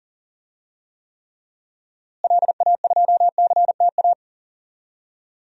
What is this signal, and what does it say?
Morse code, heard as a keyed tone, about 40 wpm, 700 Hz. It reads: LA2CTA
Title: DXCC-100 Call Signs - 6 Characters - Repeated 2x - 40wpm